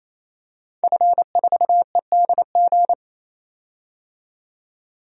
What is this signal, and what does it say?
Morse code, heard as a keyed tone, about 28 wpm, 700 Hz. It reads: F4EDG
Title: DXCC-100 Call Signs - 5 Characters - 28wpm